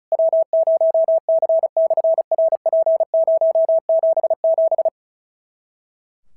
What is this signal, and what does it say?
Morse code, heard as a keyed tone, about 36 wpm, 650 Hz. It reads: W0C/RP077
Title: SOTA QSO Practice 35wpm